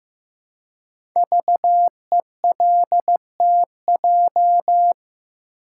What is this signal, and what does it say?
Morse code, heard as a keyed tone, about 15 wpm, 700 Hz. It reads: VELTJ
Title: Code Groups - 5 Letter:Numbers - 15wpm